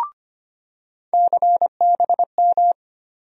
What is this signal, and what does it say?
Morse code, heard as a keyed tone, about 25 wpm, 700 Hz. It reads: CBM
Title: USA Call Sign Postfixes - 3 Letters 25wpm